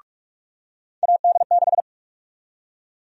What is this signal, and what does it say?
Morse code, heard as a keyed tone, about 46 wpm, 700 Hz. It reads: AD6
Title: USA Call Sign Prefixes - 45wpm